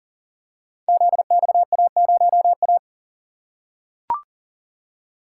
Morse code, heard as a keyed tone, about 41 wpm, 700 Hz. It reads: ZXA0A
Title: Code Groups - 5 Letter:Numbers - 40wpm